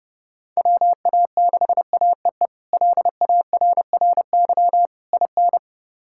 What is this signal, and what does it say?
Morse code, heard as a keyed tone, about 30 wpm, 700 Hz. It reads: WA6AEE LARRY IN
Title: K1USN SST QSO Practice - 30wpm